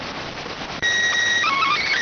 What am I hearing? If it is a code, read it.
SSTV (slow-scan television), PD 290.